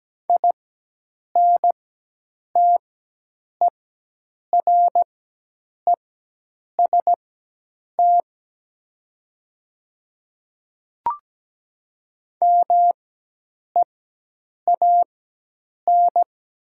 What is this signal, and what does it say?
Morse code, heard as a keyed tone, about 17 wpm, spaced as for 8 wpm, 700 Hz. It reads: INTEREST MEAN
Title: 4x Character Spacing - Top 500 Words - 17wpm